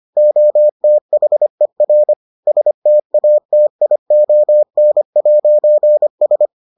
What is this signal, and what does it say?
Morse code, heard as a keyed tone, about 25 wpm, 600 Hz. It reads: OTHER STATION'S